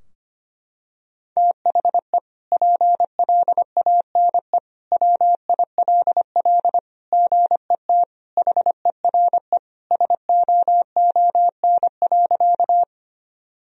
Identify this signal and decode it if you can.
Morse code, heard as a keyed tone, about 25 wpm, 700 Hz. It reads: THE PLANE WILL GET HERE SOON.
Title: Sentences from Top 1000 Words - Review Part 3 - 25wpm